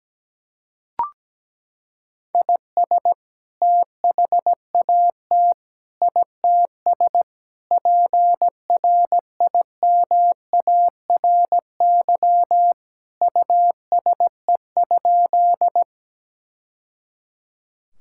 Morse code, heard as a keyed tone, about 17 wpm, 700 Hz. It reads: IS THAT ITS PRIMARY USE?
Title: Sentences from Top 1000 Words - Review Part 5 - 17wpm